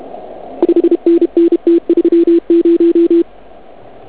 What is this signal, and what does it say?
Morse code, heard as a keyed tone, about 31 wpm, 340 Hz. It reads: 5NN T30
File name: Archiv / CUC106 / cuc106 PILEUP